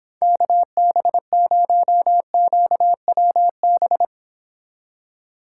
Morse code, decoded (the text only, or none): KB0QWB